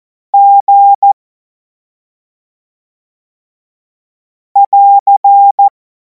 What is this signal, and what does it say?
Morse code, heard as a keyed tone, about 14 wpm, 800 Hz. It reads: G <AR>
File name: Morsekurs / Witze / Witz 03. März 2026 25 F70